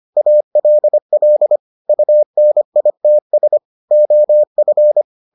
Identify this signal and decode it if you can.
Morse code, heard as a keyed tone, about 25 wpm, 600 Hz. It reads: ALL UNITS OF